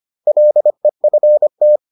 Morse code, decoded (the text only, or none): LEFT